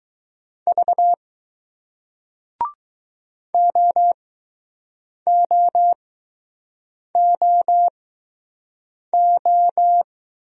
Morse code, heard as a keyed tone, about 21 wpm, spaced as for 6 wpm, 700 Hz. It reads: V OOOO